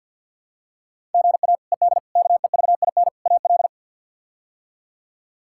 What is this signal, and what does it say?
Morse code, heard as a keyed tone, about 49 wpm, 700 Hz. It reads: GA ED KEVIN AL
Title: K1USN SST QSO Practice - 50wpm